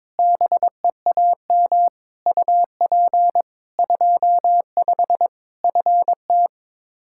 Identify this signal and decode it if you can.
Morse code, heard as a keyed tone, about 22 wpm, 700 Hz. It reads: BEAM UP 25 FT
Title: K1USN SST QSO Practice 22wpm